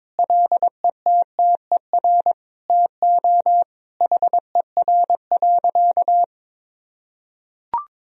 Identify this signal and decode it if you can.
Morse code, heard as a keyed tone, about 22 wpm, 700 Hz. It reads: LETTER TO HER.
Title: Sentences from Top 1000 Words - Review Part 3 - 22wpm